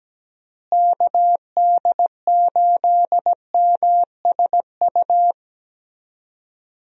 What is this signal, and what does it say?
Morse code, heard as a keyed tone, about 17 wpm, 700 Hz. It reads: KD8MSU
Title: USA Call Signs - 6 Characters - Repeated 3x - 17wpm